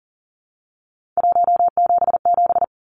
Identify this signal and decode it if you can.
Morse code, heard as a keyed tone, about 40 wpm, 700 Hz. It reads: 177
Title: Numbers - 3-Digit 40wpm